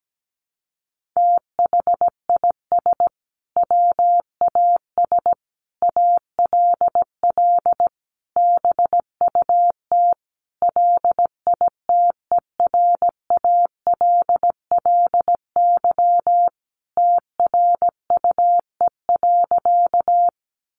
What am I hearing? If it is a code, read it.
Morse code, heard as a keyed tone, about 17 wpm, 700 Hz. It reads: THIS WAS ALL BUT LITERALLY TRUE.
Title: Sentences from Top 1400 Words 17wpm